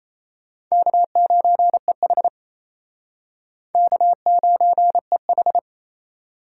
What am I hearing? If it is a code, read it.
Morse code, heard as a keyed tone, about 34 wpm, 700 Hz. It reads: K9EH K9EH
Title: Speed Racing - USA 1x2 Call Signs 28wpm